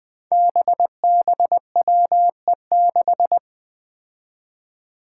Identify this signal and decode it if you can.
Morse code, heard as a keyed tone, about 21 wpm, 700 Hz. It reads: BBWE6
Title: Five Challenging Letter-Number Combinations 20wpm